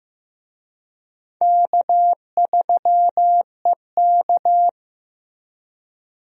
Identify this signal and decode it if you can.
Morse code, heard as a keyed tone, about 15 wpm, 700 Hz. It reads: K3EK
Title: USA Call Signs - 4 Characters - Repeated 3x - 15wpm